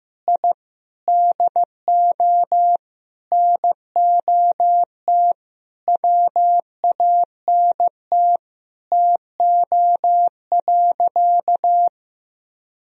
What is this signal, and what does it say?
Morse code, heard as a keyed tone, about 15 wpm, 700 Hz. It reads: I DO NOT WANT TO.
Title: Sentences from Top 1000 Words - Review Part 3 - 15wpm